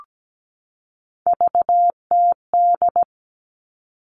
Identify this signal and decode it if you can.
Morse code, heard as a keyed tone, about 17 wpm, 700 Hz. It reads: VTD
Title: All Cut Numbers - 3-Digit 17wpm